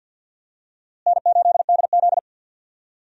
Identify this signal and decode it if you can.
Morse code, heard as a keyed tone, about 52 wpm, 700 Hz. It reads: N8DZ
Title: USA Call Signs - 4 Characters - Repeated 2x - 50wpm